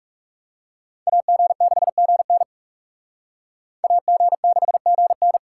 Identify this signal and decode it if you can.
Morse code, heard as a keyed tone, about 38 wpm, 700 Hz. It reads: AG6GN AG6GN
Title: Speed Racing - USA 2x2 Call Signs 30wpm